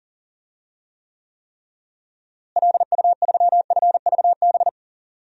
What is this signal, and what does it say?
Morse code, heard as a keyed tone, about 40 wpm, 700 Hz. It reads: LU3FVB
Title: DXCC-100 Call Signs - 6 Characters - Repeated 3x - 40wpm